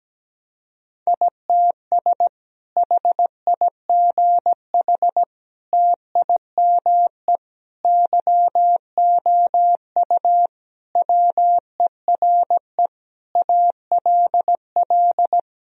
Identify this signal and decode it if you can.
Morse code, heard as a keyed tone, about 17 wpm, 700 Hz. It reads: ITS HIGH TIME YOU WERE ALL
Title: Sentences from Top 1000 Words - Review Part 1 - 17wpm